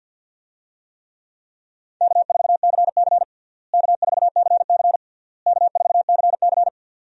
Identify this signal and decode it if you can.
Morse code, heard as a keyed tone, about 46 wpm, 700 Hz. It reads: K4CC K4CC K4CC
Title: USA Call Signs - 4 Characters - Repeated 3x - 50wpm